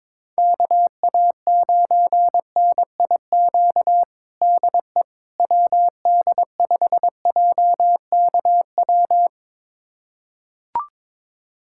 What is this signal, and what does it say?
Morse code, heard as a keyed tone, about 22 wpm, 700 Hz. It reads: KA9NIQ DE WD5JKW
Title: USA Callsign DE Callsign - 6 Characters - 22wpm